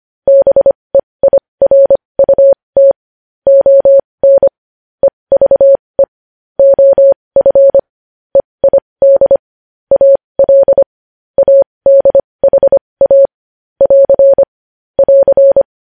Morse code, heard as a keyed tone, about 25 wpm, spaced as for 20 wpm, 550 Hz. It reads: BEIRUT ON EVE OF EID AL ADHA <AR> <AR>
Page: news-headlines-in-morse-code-at-20-wpm